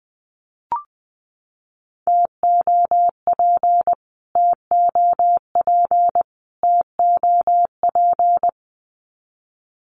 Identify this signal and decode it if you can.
Morse code, heard as a keyed tone, about 20 wpm, 700 Hz. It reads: TOP TOP TOP
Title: Top 500 Words Repeated 20wpm